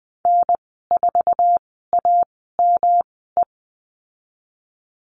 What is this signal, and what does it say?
Morse code, heard as a keyed tone, about 19 wpm, spaced as for 14 wpm, 700 Hz. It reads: N4AME
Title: 154 Lesson 25g - Letter p - Call Signs 20wpm